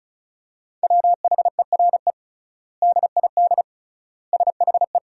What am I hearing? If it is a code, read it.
Morse code, heard as a keyed tone, about 36 wpm, 700 Hz. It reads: WHERE DID SHE
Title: 3x Word-Spacing - Sentences from Top 200 Words - 35wpm